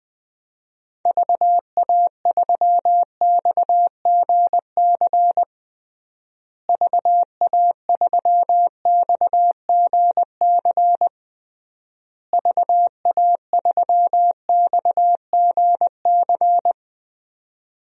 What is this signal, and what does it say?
Morse code, heard as a keyed tone, about 21 wpm, 700 Hz. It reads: VA3XGC VA3XGC VA3XGC
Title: DXCC-100 Call Signs - 6 Characters - Repeated 3x - 20wpm